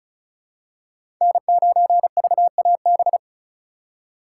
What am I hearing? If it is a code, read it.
Morse code, heard as a keyed tone, about 36 wpm, 700 Hz. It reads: N9VAB
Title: USA Call Signs - 5 Characters - 2x Character Spacing - 35wpm